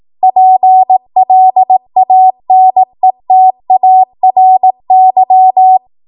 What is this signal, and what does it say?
Morse code, heard as a keyed tone, about 18 wpm, 750 Hz. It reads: PLANETARY
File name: Archive / 18 WPM / 18WPM